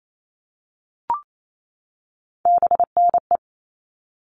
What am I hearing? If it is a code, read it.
Morse code, heard as a keyed tone, about 28 wpm, 700 Hz. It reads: BNE
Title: All Cut Numbers - 3-Digit 28wpm